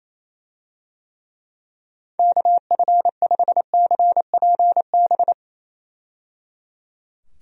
Morse code, heard as a keyed tone, about 27 wpm, 700 Hz. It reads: KF5CPB
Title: USA Call Signs - 6 Characters - 28wpm